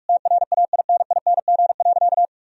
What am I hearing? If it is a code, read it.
Morse code, heard as a keyed tone, about 46 wpm, 700 Hz. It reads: TRAINING.
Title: Sentences from Top 1000 Words - Review Part 4 - 45wpm